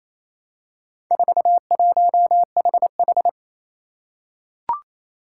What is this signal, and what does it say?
Morse code, heard as a keyed tone, about 28 wpm, 700 Hz. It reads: 41HH
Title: Four Challenging Letter-Number Combinations 28wpm